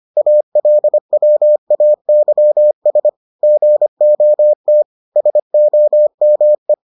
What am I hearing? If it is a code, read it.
Morse code, heard as a keyed tone, about 25 wpm, 600 Hz. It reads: ALWAYS GOT SOME